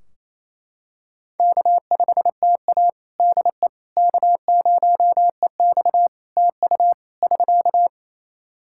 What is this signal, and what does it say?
Morse code, heard as a keyed tone, about 28 wpm, 700 Hz. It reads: K5TA DE K0EX TU <SK>
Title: Practice QSO - Exchange 4 - Ending - 28wpm